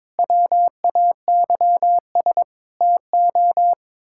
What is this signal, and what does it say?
Morse code, heard as a keyed tone, about 22 wpm, 700 Hz. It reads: WAYS TO
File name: practice / 22wpm / Sets of 7 Words - 22wpm